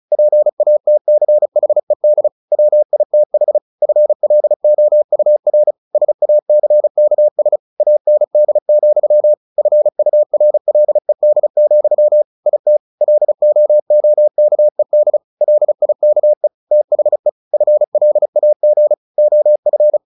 Morse code, heard as a keyed tone, about 34 wpm, 600 Hz. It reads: PATCHED WITH FLOUR SACKS AND, FURLED, IT LOOKED LIKE THE FLAG OF
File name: the old man and the sea / the old man and the sea 1 35wpm